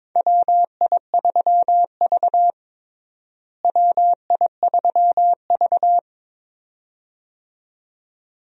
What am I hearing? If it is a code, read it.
Morse code, heard as a keyed tone, about 22 wpm, 700 Hz. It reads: WI3V WI3V
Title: USA Call Signs - 4 Characters - Repeated 2x - 22wpm